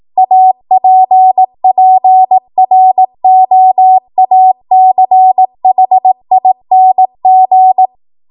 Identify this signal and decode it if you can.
Morse code, heard as a keyed tone, about 18 wpm, 750 Hz. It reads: APPROACHING
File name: Archive / 18 WPM / 18WPM